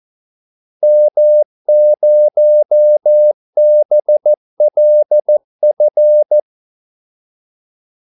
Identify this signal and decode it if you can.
Morse code, heard as a keyed tone, about 14 wpm, 600 Hz. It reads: M0BLF
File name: Callsigns-14wpm